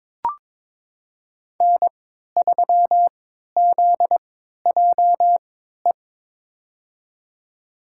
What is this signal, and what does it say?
Morse code, heard as a keyed tone, about 22 wpm, spaced as for 12 wpm, 700 Hz. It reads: N3ZJE
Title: USA Call Signs - 5 Characters - 3x Character Spacing - 22wpm